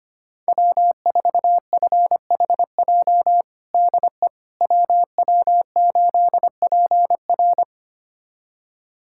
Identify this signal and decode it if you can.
Morse code, heard as a keyed tone, about 25 wpm, 700 Hz. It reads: W4FHJ DE WW8PR
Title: USA Callsign DE Callsign - 5 Characters - 25wpm